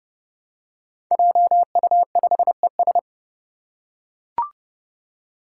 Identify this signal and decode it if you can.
Morse code, heard as a keyed tone, about 28 wpm, 700 Hz. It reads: JU5ES
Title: Five Challenging Letter-Number Combinations 30wpm